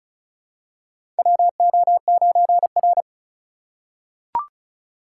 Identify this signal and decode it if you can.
Morse code, heard as a keyed tone, about 36 wpm, 700 Hz. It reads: WO9R
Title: USA Call Signs - 4 Characters - 3x Character Spacing - 35wpm